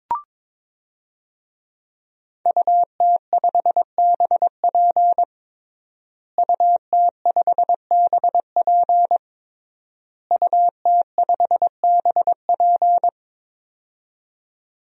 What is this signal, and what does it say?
Morse code, heard as a keyed tone, about 22 wpm, 700 Hz. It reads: UT5BP UT5BP UT5BP
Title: DXCC-100 Call Signs - 5 Characters - Repeated 3x - 22wpm